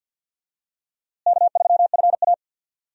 Morse code, heard as a keyed tone, about 50 wpm, 700 Hz. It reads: K3FA
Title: USA Call Signs - 4 Characters - Repeated 3x - 50wpm